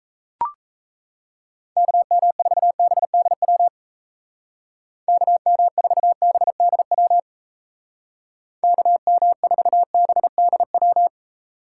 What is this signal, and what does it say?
Morse code, heard as a keyed tone, about 35 wpm, 700 Hz. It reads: KM4BDW KM4BDW KM4BDW